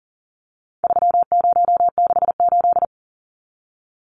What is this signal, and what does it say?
Morse code, heard as a keyed tone, about 40 wpm, 700 Hz. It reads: 3068
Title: Numbers - 4-Digit 40wpm